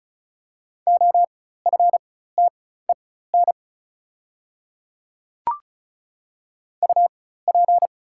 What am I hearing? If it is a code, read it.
Morse code, heard as a keyed tone, about 36 wpm, spaced as for 16 wpm, 700 Hz. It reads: OFTEN UP
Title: 4x Character Spacing - Top 300 Words - 35wpm